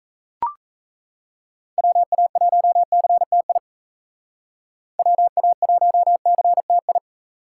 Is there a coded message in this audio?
Yes: Morse code from 1.8 to 7.0 s, heard as a keyed tone, about 36 wpm, 700 Hz. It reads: WA1CTI WA1CTI